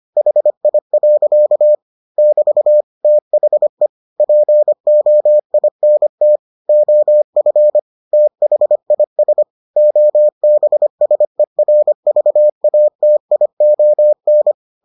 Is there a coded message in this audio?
Yes: Morse code from 0.2 to 14.5 s, heard as a keyed tone, about 25 wpm, 600 Hz. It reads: HI. <BT> THE POINT OF THIS OBSERVATION